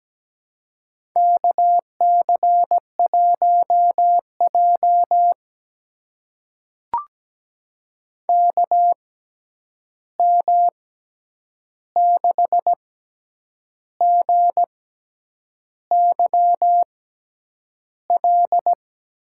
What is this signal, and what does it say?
Morse code, heard as a keyed tone, about 17 wpm, spaced as for 8 wpm, 700 Hz. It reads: KC1J KM6GYL
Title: USA Call Signs - 6x Character Spacing - 17wpm